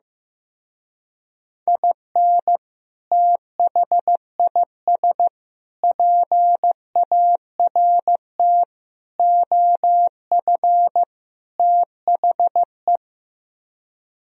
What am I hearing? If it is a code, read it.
Morse code, heard as a keyed tone, about 16 wpm, 700 Hz. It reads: IN THIS PART OF THE
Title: Speed-Racing - Sets of 5 Words 15wpm